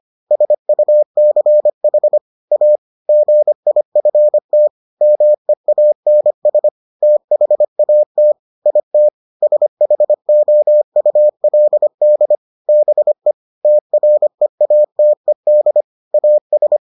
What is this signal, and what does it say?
Morse code, heard as a keyed tone, about 25 wpm, 600 Hz. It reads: SUCH A GIFT MEANS THAT IT SHOULD BE TREATED AS